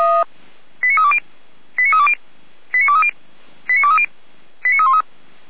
Five-tone selective call (ZVEI), heard as quick runs of five tones, tones of about 70 ms. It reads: 89320 89320 89230 89230 89313
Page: feuerwehr-orlamuende